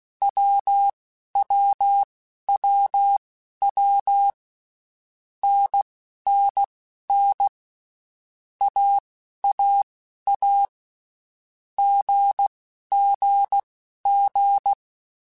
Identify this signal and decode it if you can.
Morse code, heard as a keyed tone, about 16 wpm, spaced as for 11 wpm, 800 Hz. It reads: WWWW NNN AAA GGG
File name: Basic Letter Groups By7Groups / AWUNDG / AWUNDG@10